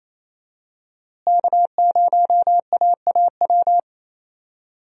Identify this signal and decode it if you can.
Morse code, heard as a keyed tone, about 27 wpm, 700 Hz. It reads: K0AAW
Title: USA Call Signs - 5 Characters - Repeated 2x - 28wpm